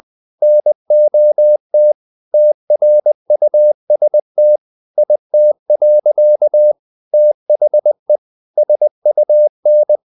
Morse code, heard as a keyed tone, about 20 wpm, 600 Hz. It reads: NOT TRUST IT. THE SUN
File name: the old man and the sea part 2 20wpm